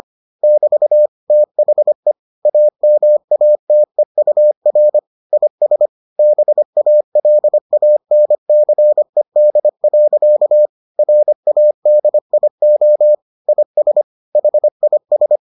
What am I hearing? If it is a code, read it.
Morse code, heard as a keyed tone, about 25 wpm, 600 Hz. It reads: <BT> THE AMATEUR IS BALANCED. RADIO IS HIS